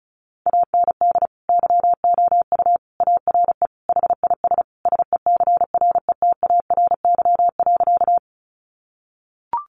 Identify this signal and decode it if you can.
Morse code, heard as a keyed tone, about 34 wpm, 700 Hz. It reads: AND YOU ARE HIS SECRETARY.
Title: Sentences from Top 1300 Words 35wpm